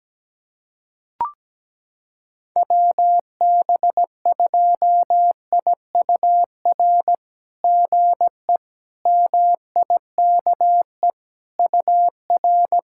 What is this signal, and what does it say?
Morse code, heard as a keyed tone, about 17 wpm, 700 Hz. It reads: WB2IUR GE MIKE UR
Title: SOTA QSO Practice 17wpm